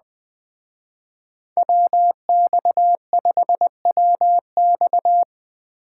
Morse code, heard as a keyed tone, about 19 wpm, 700 Hz. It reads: WX5WX